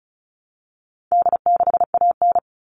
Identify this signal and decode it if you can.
Morse code, heard as a keyed tone, about 34 wpm, 700 Hz. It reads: D6AN
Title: All Cut Numbers - 4-Digit 35wpm